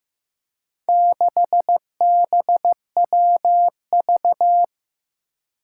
Morse code, heard as a keyed tone, about 15 wpm, 700 Hz. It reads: 6BWV
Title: Four Challenging Letter-Number Combinations 15wpm